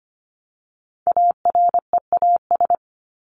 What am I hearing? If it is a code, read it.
Morse code, heard as a keyed tone, about 25 wpm, 700 Hz. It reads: AREAS
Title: ES:S Suffix - Top 100 Words with Base Words 25wpm